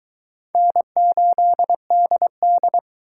Morse code, heard as a keyed tone, about 22 wpm, 700 Hz. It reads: N8DD